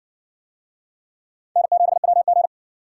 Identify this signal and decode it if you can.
Morse code, heard as a keyed tone, about 54 wpm, 700 Hz. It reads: N7WG